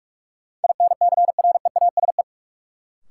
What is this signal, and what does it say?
Morse code, heard as a keyed tone, about 46 wpm, 700 Hz. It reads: INCREASE
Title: Sentences from Top 1000 Words - Review Part 4 - New Word Drill - 45wpm